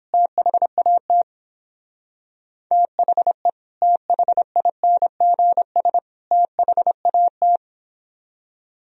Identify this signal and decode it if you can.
Morse code, heard as a keyed tone, about 31 wpm, 700 Hz. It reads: THAT THE THINGS THAT